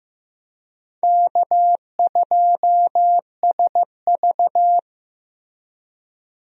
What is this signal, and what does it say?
Morse code, heard as a keyed tone, about 15 wpm, 700 Hz. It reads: K2SV
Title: USA Call Signs - 4 Characters - Repeated 3x - 15wpm